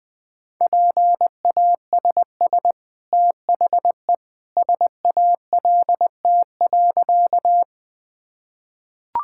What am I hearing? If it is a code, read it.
Morse code, heard as a keyed tone, about 20 wpm, 700 Hz. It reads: PASS THE SALT.
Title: Easy Sentences with Hints - 20wpm